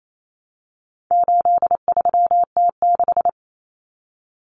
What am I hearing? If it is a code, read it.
Morse code, heard as a keyed tone, about 28 wpm, 700 Hz. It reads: 83T6
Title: Common Cut Numbers - 4-Digit 28wpm